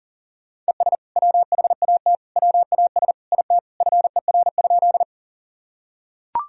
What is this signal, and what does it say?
Morse code, heard as a keyed tone, about 40 wpm, 700 Hz. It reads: ES WHAT WAS IT FER?
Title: Sentences from Top 100 Words and Abbreviations 40wpm